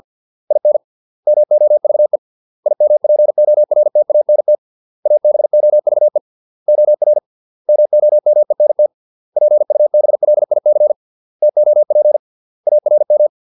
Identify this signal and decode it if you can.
Morse code, heard as a keyed tone, about 50 wpm, 600 Hz. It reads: IN MOVE IMPORTANT ABOVE OR MOMENT PUBLIC TOP ARM